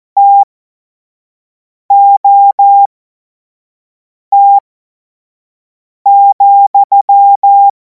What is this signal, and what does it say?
Morse code, heard as a keyed tone, about 14 wpm, spaced as for 5 wpm, 800 Hz. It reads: TOT,
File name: Morsekurs / Witze / Witz 03. Februar 2026 25 F70